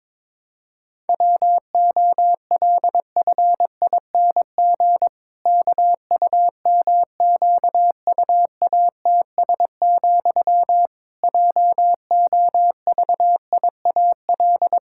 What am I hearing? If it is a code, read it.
Morse code, heard as a keyed tone, about 22 wpm, 700 Hz. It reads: WOLFING KUMQUATS, JOVIAL
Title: Pangram Challenge v2 - 22wpm